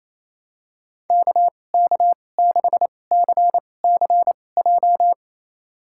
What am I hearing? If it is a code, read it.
Morse code, heard as a keyed tone, about 28 wpm, spaced as for 20 wpm, 700 Hz. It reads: KK6CCJ
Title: USA Call Signs - 2x Character Spacing - 28wpm